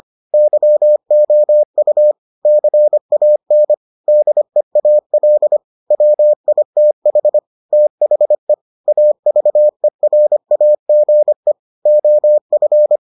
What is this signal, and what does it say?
Morse code, heard as a keyed tone, about 25 wpm, 600 Hz. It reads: YOU CAN DEAL WITH THE AVERAGE OF